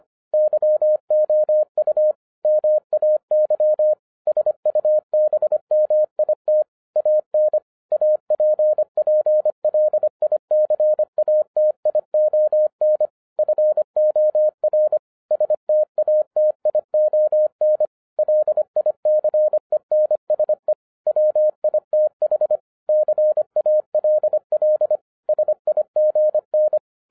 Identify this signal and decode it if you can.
Morse code, heard as a keyed tone, about 25 wpm, 610 Hz. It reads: YOU MAY SUBMIT AN APPLICATION FOR STATION LICENSE WITH CALL SIGN
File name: 2012-01-mp3 / n08-ITS MY FAULT, BUT-25wpm